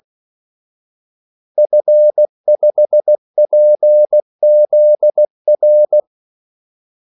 Morse code, heard as a keyed tone, about 16 wpm, 600 Hz. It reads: F5PZR
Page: Kaldesignaler 16 wpm